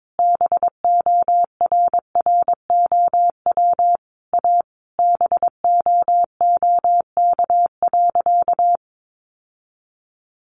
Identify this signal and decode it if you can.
Morse code, heard as a keyed tone, about 22 wpm, 700 Hz. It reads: BORROW A BOOK.
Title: Sentences from Top 1500 Words 22wpm